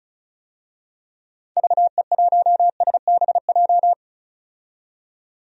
Morse code, heard as a keyed tone, about 34 wpm, 700 Hz. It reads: VE1SBJ
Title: DXCC-100 Call Signs - 6 Characters - 35wpm